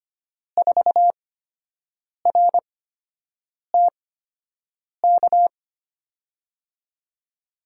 Morse code, heard as a keyed tone, about 25 wpm, spaced as for 7 wpm, 700 Hz. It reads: 4RTK
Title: USA Call Signs - 6 Characters - 8x Character Spacing - 25wpm